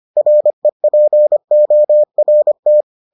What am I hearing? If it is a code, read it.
Morse code, heard as a keyed tone, about 25 wpm, 600 Hz. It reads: REPORT